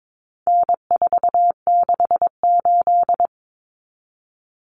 Morse code, heard as a keyed tone, about 22 wpm, 700 Hz. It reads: N468
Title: Common Cut Numbers - 4-Digit 22wpm